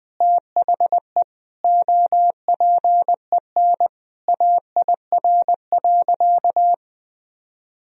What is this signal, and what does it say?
Morse code, heard as a keyed tone, about 20 wpm, 700 Hz. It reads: THE OPEN AIR.
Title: Sentences from Top 1000 Words - Review Part 1 - 20wpm